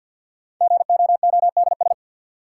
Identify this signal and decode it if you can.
Morse code, heard as a keyed tone, about 52 wpm, 700 Hz. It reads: GOODS
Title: Sentences from Top 1000 Words - Review Part 6 - New Word Drill - 50wpm